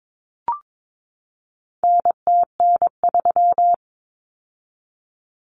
Morse code, heard as a keyed tone, about 22 wpm, 700 Hz. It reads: NTN3
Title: Common Cut Numbers - 4-Digit 22wpm